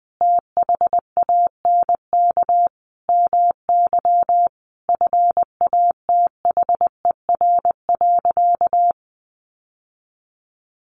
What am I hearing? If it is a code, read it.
Morse code, heard as a keyed tone, about 20 wpm, 700 Hz. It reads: THANK MY FATHER.
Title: Sentences from Top 1100 Words 20wpm